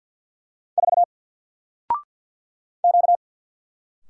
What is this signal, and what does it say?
Morse code, heard as a keyed tone, about 48 wpm, 700 Hz. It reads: 4 Q
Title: Single Character (Letter:Number:Punctuation:Prosign) Alt 50wpm